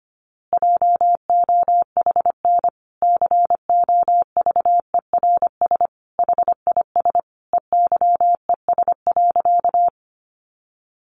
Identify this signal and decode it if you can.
Morse code, heard as a keyed tone, about 25 wpm, 700 Hz. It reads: JOHN COVERS HIS EYES.
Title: Easy Phrases 25wpm